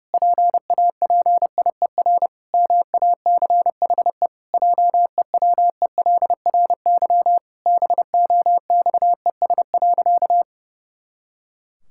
Morse code, heard as a keyed tone, about 28 wpm, 700 Hz. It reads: PAPIER MACHE JEWELRY BOXES.
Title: Pangram Challenge v2 - 30wpm